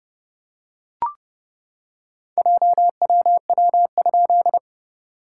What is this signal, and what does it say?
Morse code, heard as a keyed tone, about 30 wpm, 690 Hz. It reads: JWW?
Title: SOTA QSO Practice 30wpm